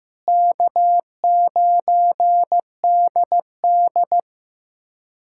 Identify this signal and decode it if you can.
Morse code, heard as a keyed tone, about 15 wpm, 700 Hz. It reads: K9DD
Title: USA Callsign DE Callsign - 4 Characters - 15wpm